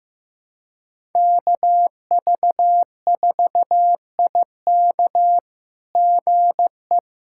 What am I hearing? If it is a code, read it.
Morse code, heard as a keyed tone, about 14 wpm, 700 Hz. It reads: KV4IK GE
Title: SOTA QSO Practice 15wpm